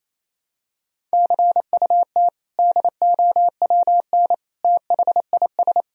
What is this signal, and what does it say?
Morse code, heard as a keyed tone, about 28 wpm, 700 Hz. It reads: CUT DOWN THIS
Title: Sentences from Top 1000 Words - Review Part 1 - 28wpm